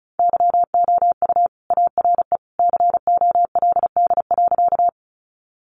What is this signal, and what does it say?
Morse code, heard as a keyed tone, about 34 wpm, 700 Hz. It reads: YOU ARE COLD.
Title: Sentences from Top 400 Words 35wpm